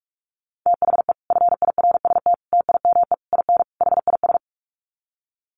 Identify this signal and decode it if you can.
Morse code, heard as a keyed tone, about 44 wpm, 700 Hz. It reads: THE FIRST TIME IN HIS
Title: Sets of 5 Words Encore2 45wpm